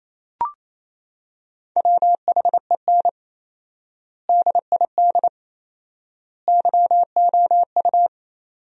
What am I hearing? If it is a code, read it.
Morse code, heard as a keyed tone, about 27 wpm, 700 Hz. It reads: WHEN DID YOU
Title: 4x Word-Spacing - Sentences from Top 300 Words - 28wpm